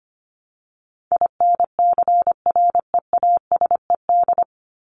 Morse code, heard as a keyed tone, about 25 wpm, 700 Hz. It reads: INCREASED
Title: ED Suffix - Top 100 Words with Base Words 25wpm